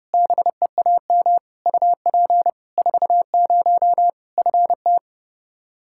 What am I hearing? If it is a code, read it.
Morse code, heard as a keyed tone, about 30 wpm, 700 Hz. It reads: BEAM UP 40 FT
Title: Practice QSO - Exchange 2 - Info - 30wpm